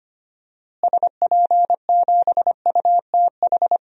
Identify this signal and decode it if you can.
Morse code, heard as a keyed tone, about 25 wpm, 700 Hz. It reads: SP7UTH
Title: DXCC-100 Call Signs - 6 Characters - Repeated 2x - 25wpm